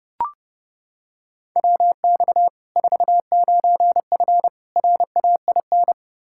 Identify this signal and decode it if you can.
Morse code, heard as a keyed tone, about 30 wpm, 700 Hz. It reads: WX 49F RAIN
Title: Practice QSO - Exchange 2 - Info - 30wpm